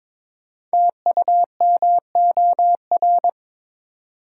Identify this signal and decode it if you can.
Morse code, heard as a keyed tone, about 22 wpm, 700 Hz. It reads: TUMOR
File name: 5 Letter Words - First 3 Letters Given - 22wpm